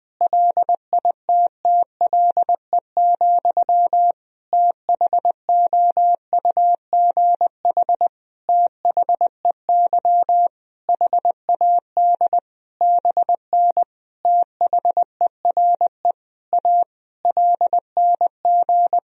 Morse code, heard as a keyed tone, about 20 wpm, 700 Hz. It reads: LITTLE, THOUGH THEY HAD BN THERE A LNG
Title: Sentences from Top 400 Words and Abbreviations 20wpm